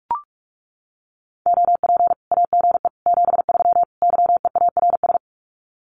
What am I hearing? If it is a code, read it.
Morse code, heard as a keyed tone, about 44 wpm, 700 Hz. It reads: OP AGE 73 YEARS
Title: Common QSO Phrases 45wpm